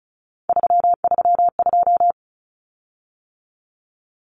Morse code, heard as a keyed tone, about 34 wpm, 700 Hz. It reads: 332
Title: Common Cut Numbers - 3-Digit 35wpm